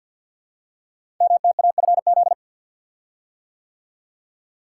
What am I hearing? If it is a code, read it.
Morse code, heard as a keyed tone, about 48 wpm, 700 Hz. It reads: MTAFZ
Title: Code Groups - 5 Letters - 50wpm